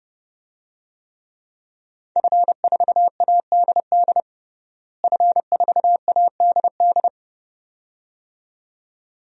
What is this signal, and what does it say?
Morse code, heard as a keyed tone, about 30 wpm, 700 Hz. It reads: F4ADD F4ADD
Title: DXCC-100 Call Signs - 5 Characters - Repeated 2x - 30wpm